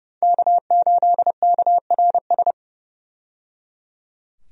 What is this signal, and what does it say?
Morse code, heard as a keyed tone, about 30 wpm, 700 Hz. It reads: K8KRS
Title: USA Call Signs - 5 Characters - 30wpm